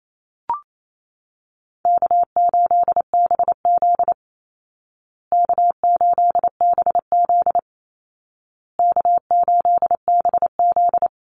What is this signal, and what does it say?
Morse code, heard as a keyed tone, about 28 wpm, 700 Hz. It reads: K8BZ K8BZ K8BZ
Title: Easy Call Signs Repeated 3x - 28wpm